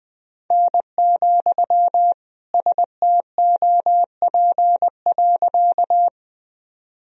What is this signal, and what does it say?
Morse code, heard as a keyed tone, about 20 wpm, 700 Hz. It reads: N, STOP.
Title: Sentences from Top 300 Words and Abbreviations 20wpm